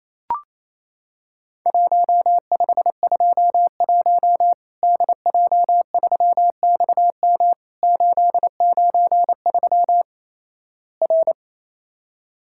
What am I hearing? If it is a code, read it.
Morse code, heard as a keyed tone, about 28 wpm, 700 Hz. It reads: 1521 DJ3XM 893 R
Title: Worked All Europe DX Contest - QTC Exchange - 28wpm